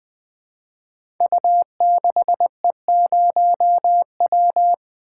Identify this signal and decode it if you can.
Morse code, heard as a keyed tone, about 20 wpm, 700 Hz. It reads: U6E0W
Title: Code Groups - 5 Letter:Numbers - 20wpm